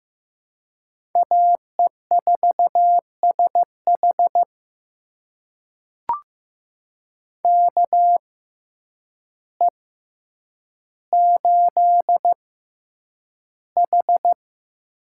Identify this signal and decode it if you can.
Morse code, heard as a keyed tone, about 15 wpm, spaced as for 8 wpm, 700 Hz. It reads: AE4SH KE8H
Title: USA Call Signs - 5 Characters - 6x Character Spacing - 15wpm